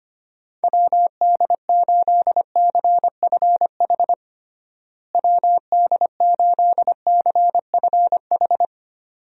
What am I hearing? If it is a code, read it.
Morse code, heard as a keyed tone, about 25 wpm, 700 Hz. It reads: WD8CFH WD8CFH
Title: USA Call Signs - 6 Characters - Repeated 2x - 25wpm